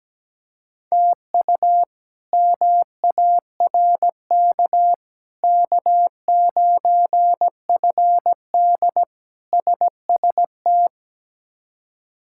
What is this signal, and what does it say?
Morse code, heard as a keyed tone, about 17 wpm, 700 Hz. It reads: TU MARK K9FD SST
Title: K1USN SST QSO Practice - 17wpm